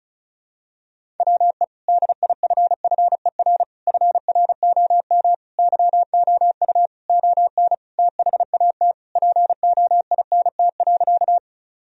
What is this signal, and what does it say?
Morse code, heard as a keyed tone, about 34 wpm, 700 Hz. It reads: WE DIFFER FROM YOU ON THAT POINT.
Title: Sentences from Top 1000 Words - Review Part 1 - 35wpm